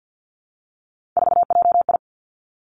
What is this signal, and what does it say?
Morse code, heard as a keyed tone, about 50 wpm, 700 Hz. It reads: 4JI
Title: Three Challenging Letter-Number Combinations 50wpm